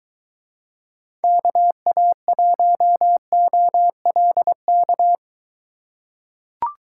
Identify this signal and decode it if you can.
Morse code, heard as a keyed tone, about 22 wpm, 700 Hz. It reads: KA1OLK